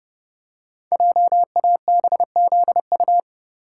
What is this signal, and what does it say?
Morse code, heard as a keyed tone, about 31 wpm, 700 Hz. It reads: JABZU
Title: Code Groups - 5 Letters - 30wpm